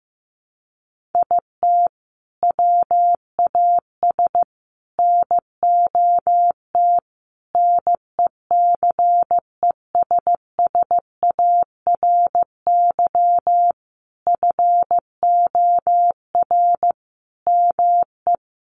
Morse code, heard as a keyed tone, about 15 wpm, 700 Hz. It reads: IT WAS NOT NECESSARY FOR ME
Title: Sentences from Top 1300 Words 15wpm